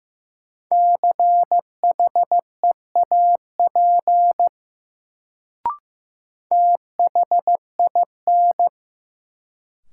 Morse code, heard as a keyed tone, about 15 wpm, 700 Hz. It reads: CHEAP THIN
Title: Sentences from Top 1200 Words - New Word Drill - 15wpm